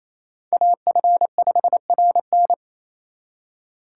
Morse code, heard as a keyed tone, about 35 wpm, 700 Hz. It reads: AF5RN